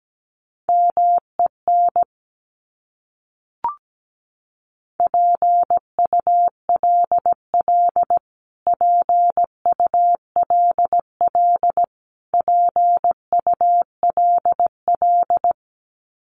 Morse code, heard as a keyed tone, about 17 wpm, 700 Hz. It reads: MEN PULL PULL PULL
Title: Top 500 Words Repeated 17wpm